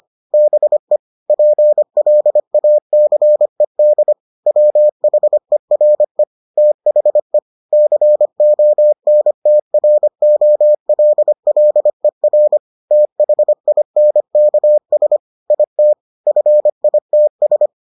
Morse code, heard as a keyed tone, about 25 wpm, 600 Hz. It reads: BE PLACED WHERE THE CONTROLLER THINKS IT FITS